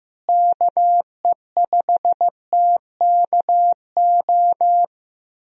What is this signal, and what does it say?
Morse code, heard as a keyed tone, about 15 wpm, 700 Hz. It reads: KE5TKO
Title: USA Call Signs - 6 Characters - Repeated 2x - 15wpm